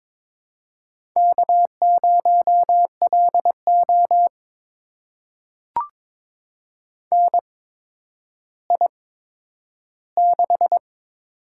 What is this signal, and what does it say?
Morse code, heard as a keyed tone, about 22 wpm, spaced as for 11 wpm, 700 Hz. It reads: K0LO NI6
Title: USA Call Signs - 4 Characters - 8x Character Spacing - 22wpm